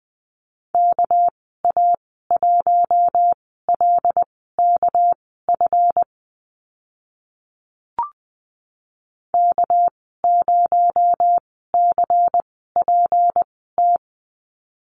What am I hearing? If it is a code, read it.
Morse code, heard as a keyed tone, about 20 wpm, spaced as for 14 wpm, 700 Hz. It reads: KA1LKF K0CPT
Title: 245 Lesson 39g - Letter z - Call Signs 20wpm